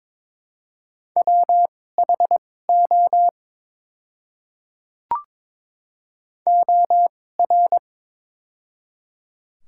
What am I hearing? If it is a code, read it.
Morse code, heard as a keyed tone, about 23 wpm, spaced as for 16 wpm, 700 Hz. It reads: WHO OR
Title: 2x Character Spacing - Top 100 Words - 22wpm